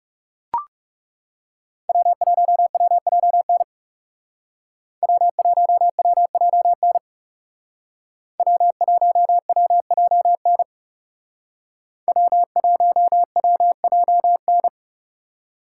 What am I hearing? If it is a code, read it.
Morse code, heard as a keyed tone, about 38 wpm, 700 Hz. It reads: W1WJN W1WJN W1WJN W1WJN